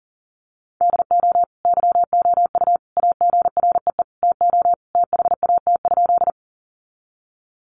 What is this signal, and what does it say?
Morse code, heard as a keyed tone, about 40 wpm, 700 Hz. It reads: DO YOU AGREE TO THAT?
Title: Sentences from Top 1100 Words 40wpm